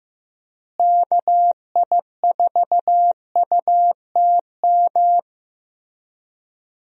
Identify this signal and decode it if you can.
Morse code, heard as a keyed tone, about 15 wpm, 700 Hz. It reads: KI4UTM
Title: USA Call Signs - 6 Characters - Repeated 3x - 15wpm